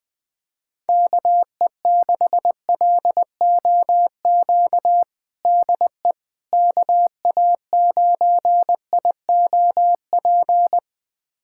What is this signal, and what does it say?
Morse code, heard as a keyed tone, about 20 wpm, 700 Hz. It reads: KE6LOQ DE KA9IOP
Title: USA Callsign DE Callsign 20wpm